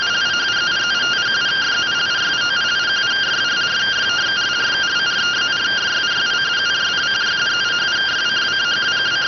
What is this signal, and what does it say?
Two-tone teleprinter signal, about 170 Hz shift.